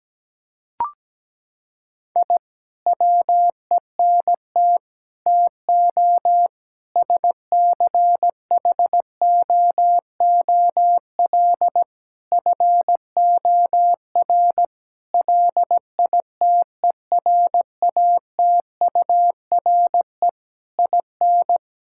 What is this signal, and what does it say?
Morse code, heard as a keyed tone, about 17 wpm, 700 Hz. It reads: I WENT TO SCHOOL FOR LITERATURE IN
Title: Sentences from Top 1000 Words - Review Part 6 - 17wpm